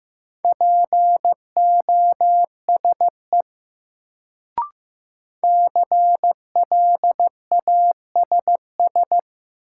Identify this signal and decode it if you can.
Morse code, heard as a keyed tone, about 15 wpm, 700 Hz. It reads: POSE CLASS
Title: Sentences from Top 1000 Words - Review Part 2 - New Word Drill - 15wpm